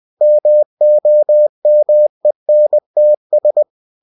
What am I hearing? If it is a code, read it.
Morse code, heard as a keyed tone, about 20 wpm, 600 Hz. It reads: MOMENTS
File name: cw-books / winnie-the-pooh / winnie-20wpm-page-0070